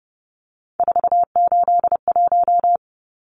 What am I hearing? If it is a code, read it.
Morse code, heard as a keyed tone, about 30 wpm, 700 Hz. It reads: 481
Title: Numbers - 3-Digit 30wpm